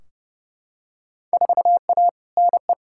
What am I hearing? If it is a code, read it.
Morse code, heard as a keyed tone, about 30 wpm, 700 Hz. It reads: 4A NE
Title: Field Day Exchanges - 30wpm